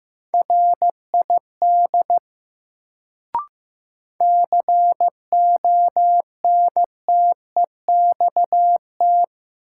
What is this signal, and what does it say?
Morse code, heard as a keyed tone, about 15 wpm, 700 Hz. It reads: RID CONTEXT
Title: Sets of 4 Words - New Word Drill - 15wpm